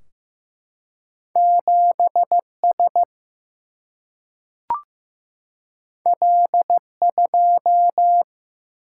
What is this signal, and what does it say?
Morse code, heard as a keyed tone, about 15 wpm, 700 Hz. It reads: 7 S L 2
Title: Single-Letter Single-Number 15wpm